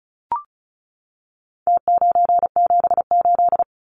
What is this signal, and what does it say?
Morse code, heard as a keyed tone, about 34 wpm, 700 Hz. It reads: T978
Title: Numbers - Contest Style - 4-Digit 35wpm